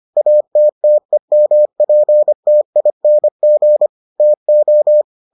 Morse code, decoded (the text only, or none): ATTEMPTING TO